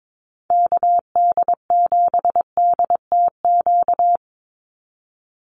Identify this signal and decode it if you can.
Morse code, heard as a keyed tone, about 22 wpm, 700 Hz. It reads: KD7DTQ
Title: USA Call Signs 22wpm